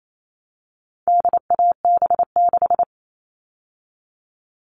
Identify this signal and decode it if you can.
Morse code, heard as a keyed tone, about 28 wpm, 700 Hz. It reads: DAB6
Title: All Cut Numbers - 4-Digit 28wpm